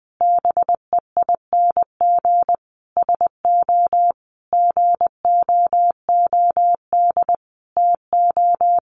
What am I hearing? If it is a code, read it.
Morse code, heard as a keyed tone, about 20 wpm, 700 Hz. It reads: BEING SO GOOD TO
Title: Sentences from Top 1400 Words 20wpm